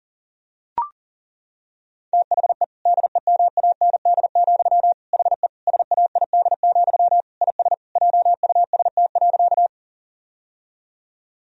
Morse code, heard as a keyed tone, about 40 wpm, 700 Hz. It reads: THE DEMAND, HE SAID, IS JUST.
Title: Sentences from Top 1000 Words - Review Part 4 - 40wpm